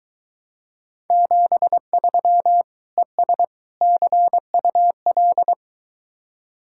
Morse code, heard as a keyed tone, about 23 wpm, 700 Hz. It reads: 73 ES CUL